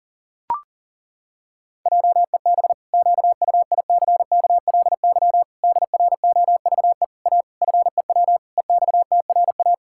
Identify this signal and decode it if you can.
Morse code, heard as a keyed tone, about 37 wpm, 700 Hz. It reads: JEB QUICKLY DROVE A FEW EXTRA
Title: Pangram Challenge v2 - 40wpm